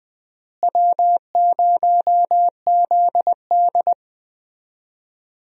Morse code, heard as a keyed tone, about 21 wpm, 700 Hz. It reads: W0ZD
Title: USA Call Signs - 4 Characters - Repeated 2x - 20wpm